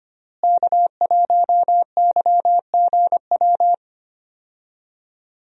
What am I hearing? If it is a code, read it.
Morse code, heard as a keyed tone, about 25 wpm, 700 Hz. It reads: K1YGW
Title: USA Call Signs - 5 Characters - Repeated 3x - 25wpm